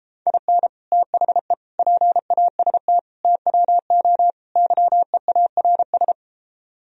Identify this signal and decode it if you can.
Morse code, heard as a keyed tone, about 31 wpm, 700 Hz. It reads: IN THE PAST TWO YEARS